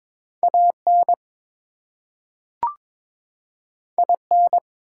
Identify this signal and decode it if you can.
Morse code, heard as a keyed tone, about 22 wpm, 700 Hz. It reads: AN IN
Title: Two Letter Words - 22wpm